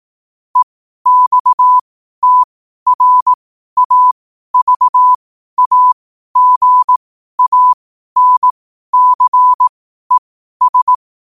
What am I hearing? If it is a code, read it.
Morse code, heard as a keyed tone, about 18 wpm, spaced as for 12 wpm, 1000 Hz. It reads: EXTRAVAGANCES
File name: qotdfiles / 12wpm